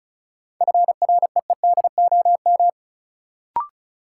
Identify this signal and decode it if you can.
Morse code, heard as a keyed tone, about 34 wpm, 700 Hz. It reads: FREEDOM
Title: Sentences from Top 800 Words - New Word Drill - 35wpm